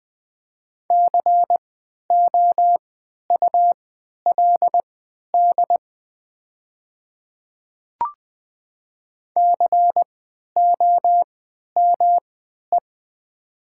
Morse code, heard as a keyed tone, about 20 wpm, spaced as for 11 wpm, 700 Hz. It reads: COULD COME
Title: 3x Character Spacing - Top 100 Words - 20wpm